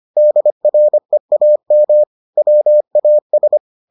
Morse code, heard as a keyed tone, about 25 wpm, 600 Hz. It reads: DREAM WAS